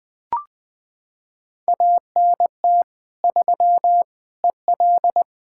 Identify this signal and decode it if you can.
Morse code, heard as a keyed tone, about 20 wpm, 700 Hz. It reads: ANT 3 EL
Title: Practice QSO - Exchange 2 - Info - 20wpm